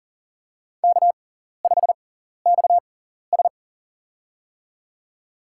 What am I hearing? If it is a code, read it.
Morse code, heard as a keyed tone, about 40 wpm, spaced as for 14 wpm, 700 Hz. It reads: K5XS
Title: USA Call Signs - 4 Characters - 6x Character Spacing - 40wpm